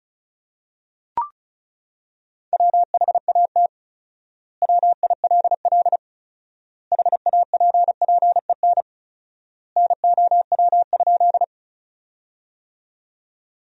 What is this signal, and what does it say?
Morse code, heard as a keyed tone, about 36 wpm, 700 Hz. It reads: WHAT WILL HAPPEN NOW?
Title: 4x Word-Spacing - Sentences from Top 400 Words - 35wpm